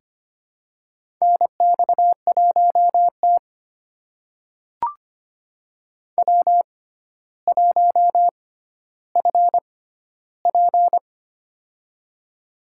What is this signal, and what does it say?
Morse code, heard as a keyed tone, about 25 wpm, spaced as for 13 wpm, 700 Hz. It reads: NX1T W1FP
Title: USA Call Signs - 4 Characters - 6x Character Spacing - 25wpm